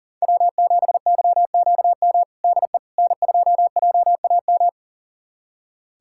Morse code, heard as a keyed tone, about 37 wpm, 700 Hz. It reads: W7YQM DE N2JAM
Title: USA Callsign DE Callsign - 5 Characters - 40wpm